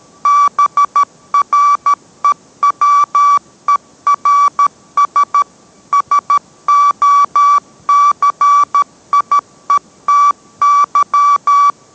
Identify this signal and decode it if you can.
Morse code, heard as a keyed tone, about 13 wpm, 1150 Hz. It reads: BREWERS SOCIETY